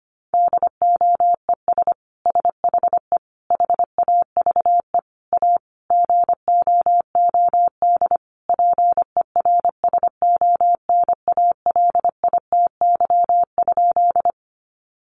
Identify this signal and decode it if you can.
Morse code, heard as a keyed tone, about 25 wpm, 700 Hz. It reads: DOES SHE HAVE A GOOD PERSONALITY?
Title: Sentences from Top 1400 Words 25wpm